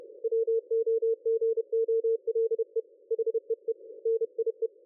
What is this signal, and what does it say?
Morse code, heard as a keyed tone, about 31 wpm, 450 Hz. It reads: WOGOLE HEE NIE
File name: dlazaawansowanych / nagraniazpasma / ExpertSDR2 line out 23.01.2020 14.38.27